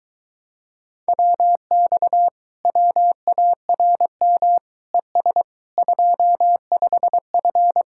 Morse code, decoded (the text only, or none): WX WARM ES 25F